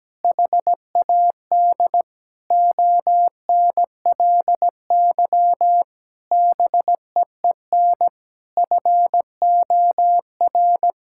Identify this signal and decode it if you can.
Morse code, heard as a keyed tone, about 17 wpm, 700 Hz. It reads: HAD ONLY BEEN FOR
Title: Sentences from Top 1600 Words 17wpm